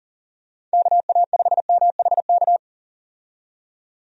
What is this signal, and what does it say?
Morse code, heard as a keyed tone, about 40 wpm, 700 Hz. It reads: KA5MHK
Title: USA Call Signs - 6 Characters - 40wpm